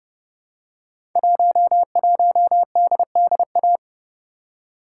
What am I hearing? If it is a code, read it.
Morse code, heard as a keyed tone, about 29 wpm, 700 Hz. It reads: 11DDA
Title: Code Groups - 5 Letter:Numbers - 30wpm